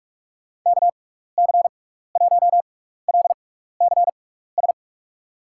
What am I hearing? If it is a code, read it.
Morse code, heard as a keyed tone, about 45 wpm, spaced as for 15 wpm, 700 Hz. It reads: KC1LCS
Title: USA Call Signs - 6 Characters - 6x Character Spacing - 45wpm